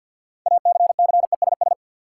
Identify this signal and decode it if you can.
Morse code, heard as a keyed tone, about 52 wpm, 700 Hz. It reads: ACCESS
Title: Sentences from Top 1000 Words - Review Part 4 - New Word Drill - 50wpm